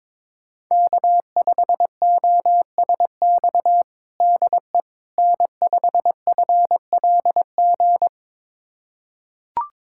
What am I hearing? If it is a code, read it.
Morse code, heard as a keyed tone, about 22 wpm, 700 Hz. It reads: K5OSX DE N5FLG
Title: USA Callsign DE Callsign - 5 Characters - 22wpm